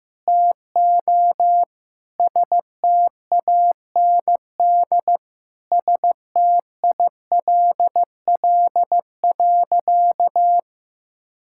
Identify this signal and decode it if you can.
Morse code, heard as a keyed tone, about 15 wpm, 700 Hz. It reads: TO STAND STILL.
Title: Sentences from Top 1000 Words - Review Part 6 - 15wpm